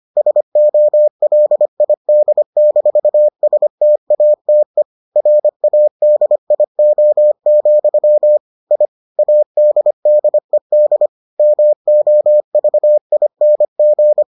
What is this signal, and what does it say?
Morse code, heard as a keyed tone, about 25 wpm, 600 Hz. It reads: SOLID-STATE RADIO, I ADDED MOVING